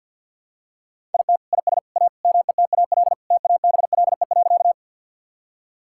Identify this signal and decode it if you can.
Morse code, heard as a keyed tone, about 50 wpm, 700 Hz. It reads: IT IS A METAL TABLE.
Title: Sentences from Top 1000 Words - Review Part 6 - 50wpm